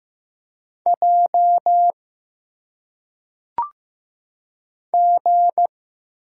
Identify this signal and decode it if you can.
Morse code, heard as a keyed tone, about 15 wpm, 700 Hz. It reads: J G
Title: Single Letters - Phonetic Pronunciation 15wpm